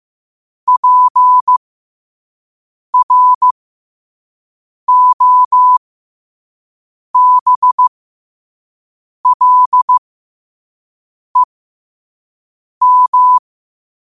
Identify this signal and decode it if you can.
Morse code, heard as a keyed tone, about 15 wpm, spaced as for 5 wpm, 1000 Hz. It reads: PROBLEM
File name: qotdfiles / 5wpm